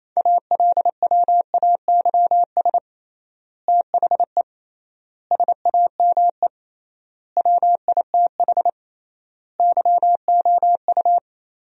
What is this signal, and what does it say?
Morse code, heard as a keyed tone, about 27 wpm, 700 Hz. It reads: ALWAYS THE SAME WITH YOU
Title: 3x Word-Spacing - Sentences from Top 300 Words - 28wpm